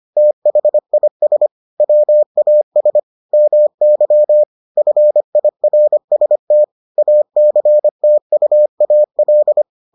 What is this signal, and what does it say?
Morse code, heard as a keyed tone, about 25 wpm, 600 Hz. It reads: THIS WAS MY FIRST ACTUAL